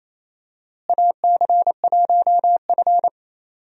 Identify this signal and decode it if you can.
Morse code, heard as a keyed tone, about 27 wpm, 700 Hz. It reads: AC1F
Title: USA Call Signs - 4 Characters - Repeated 2x - 28wpm